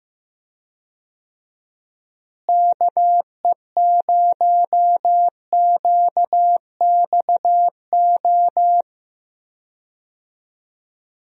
Morse code, heard as a keyed tone, about 15 wpm, 700 Hz. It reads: KE0QXO
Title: USA Call Signs - 6 Characters - Repeated 3x - 15wpm